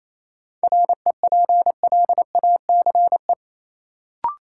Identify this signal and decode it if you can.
Morse code, heard as a keyed tone, about 28 wpm, 700 Hz. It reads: REPLACE
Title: Sentences from Top 1400 Words - New Word Drill Alt - 28wpm